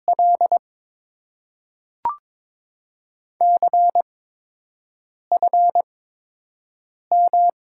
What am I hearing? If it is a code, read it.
Morse code, heard as a keyed tone, about 22 wpm, spaced as for 6 wpm, 700 Hz. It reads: L CFM
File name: Top 90 QSO Elements - 8x Character Spacing - 22wpm